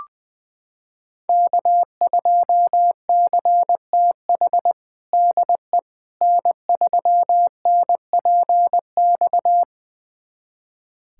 Morse code, heard as a keyed tone, about 20 wpm, 700 Hz. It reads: K2CTH DE N3NPX
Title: USA Callsign DE Callsign - 5 Characters - 20wpm